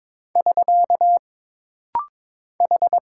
Morse code, heard as a keyed tone, about 22 wpm, 700 Hz. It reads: <SK> H
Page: This rapid-fire timing is geared to push your ICR (Instant Character Recognition) skills.